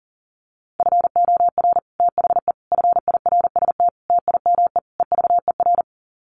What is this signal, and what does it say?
Morse code, heard as a keyed tone, about 40 wpm, 700 Hz. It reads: FOR THE FIRST TIME EVER
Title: Sets of 5 Words Encore2 40wpm